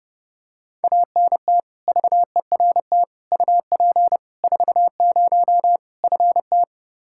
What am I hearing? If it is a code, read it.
Morse code, heard as a keyed tone, about 32 wpm, 700 Hz. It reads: ANT VERT UP 40 FT
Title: Practice QSO - Exchange 2 - Info - 30wpm